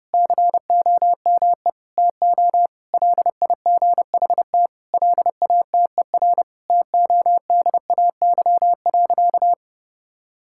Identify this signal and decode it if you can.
Morse code, heard as a keyed tone, about 30 wpm, 700 Hz. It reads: COME TO LIGHT LATER TODAY.
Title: Sentences from Top 1000 Words - Review Part 6 - 30wpm